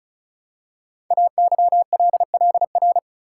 Morse code, heard as a keyed tone, about 36 wpm, 700 Hz. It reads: AYLLR
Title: Code Groups - 5 Letters - 35wpm